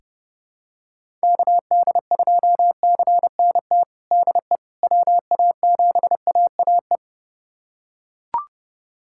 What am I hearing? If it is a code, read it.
Morse code, heard as a keyed tone, about 30 wpm, 700 Hz. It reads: KD2CNT DE WA7AAE
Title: USA Callsign DE Callsign - 6 Characters - 30wpm